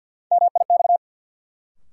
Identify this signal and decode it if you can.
Morse code, heard as a keyed tone, about 50 wpm, 700 Hz. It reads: MIX
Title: Sentences from Top 1000 Words - Review Part 4 - New Word Drill - 50wpm